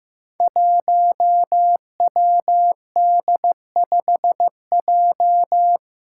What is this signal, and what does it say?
Morse code, heard as a keyed tone, about 15 wpm, 700 Hz. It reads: 1WD5J
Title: Five Challenging Letter-Number Combinations 15wpm